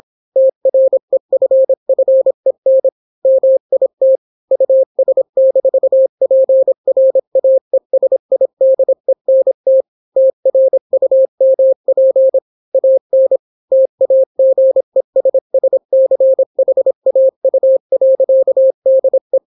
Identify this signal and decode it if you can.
Morse code, heard as a keyed tone, about 26 wpm, 530 Hz. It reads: TREFFEN MIT US-PRAESIDENT TRUMP AN TAGESSCHAU.DE